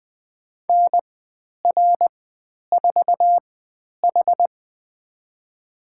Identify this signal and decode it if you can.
Morse code, heard as a keyed tone, about 20 wpm, spaced as for 10 wpm, 700 Hz. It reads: NR4H
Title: 109 Lesson 18g - Letter m - Call Signs 20wpm